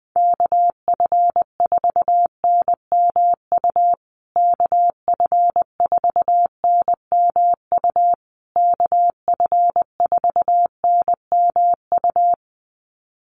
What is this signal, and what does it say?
Morse code, heard as a keyed tone, about 20 wpm, 700 Hz. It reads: KF4NMU KF4NMU KF4NMU
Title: USA Call Signs Repeated 3x 20wpm